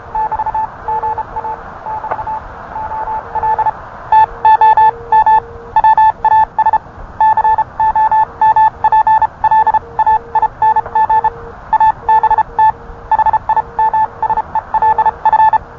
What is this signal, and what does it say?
Morse code, heard as a keyed tone, about 31 wpm, spaced as for 26 wpm, 870 Hz. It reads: <BT>GA <BT> OL TOM WAS COMPLAING ABT HIMSELF